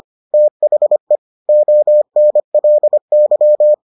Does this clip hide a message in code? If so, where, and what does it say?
Yes: Morse code from 0.3 to 3.7 s, heard as a keyed tone, about 25 wpm, 600 Hz. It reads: THE ONLY